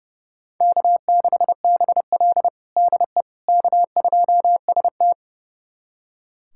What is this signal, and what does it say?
Morse code, heard as a keyed tone, about 31 wpm, 700 Hz. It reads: K6BL DE K2ST
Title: USA Callsign DE Callsign - 4 Characters - 30wpm